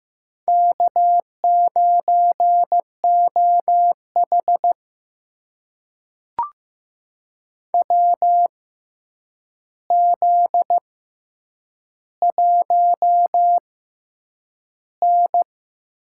Morse code, heard as a keyed tone, about 15 wpm, spaced as for 8 wpm, 700 Hz. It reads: K9OH WZ1N
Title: USA Call Signs - 4 Characters - 6x Character Spacing - 15wpm